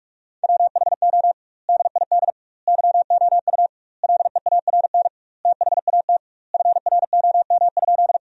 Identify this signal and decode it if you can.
Morse code, heard as a keyed tone, about 46 wpm, 700 Hz. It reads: WHO DID YOU LEARN THAT FROM?
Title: 2x Word-Spacing - Sentences from Top 300 Words - 45wpm